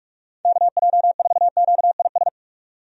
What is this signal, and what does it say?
Morse code, heard as a keyed tone, about 48 wpm, 700 Hz. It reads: KJ4QIS
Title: USA Call Signs - 6 Characters - 45wpm